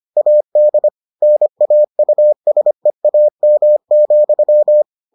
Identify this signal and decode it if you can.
Morse code, heard as a keyed tone, about 25 wpm, 600 Hz. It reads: AD NAUSEAM,